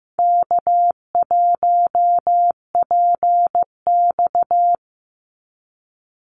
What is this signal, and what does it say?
Morse code, heard as a keyed tone, about 15 wpm, 700 Hz. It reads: K1PX
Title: Easy Call Signs Repeated 3x - 15wpm